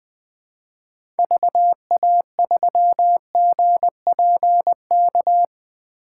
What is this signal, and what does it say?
Morse code, heard as a keyed tone, about 20 wpm, 700 Hz. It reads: VA3GPK
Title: 2023 POTA Activator Call Signs - 20wpm